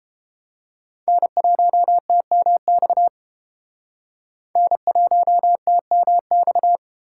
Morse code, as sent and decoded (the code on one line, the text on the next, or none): -. .---- - -- -..- / -. .---- - -- -..-
N1TMX N1TMX